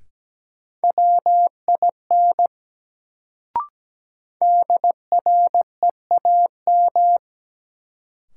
Morse code, heard as a keyed tone, about 17 wpm, 700 Hz. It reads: WIN DREAM
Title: Sentences from Top 1500 Words - New Word Drill Alt - 17wpm